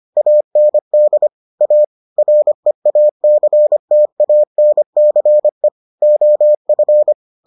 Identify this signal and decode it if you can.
Morse code, heard as a keyed tone, about 25 wpm, 600 Hz. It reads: AND A REACTANCE OF